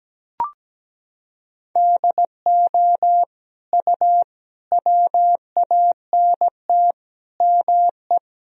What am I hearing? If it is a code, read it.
Morse code, heard as a keyed tone, about 17 wpm, 700 Hz. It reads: DO U WANT ME
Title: Sentences from Top 200 Words and Abbreviations 17wpm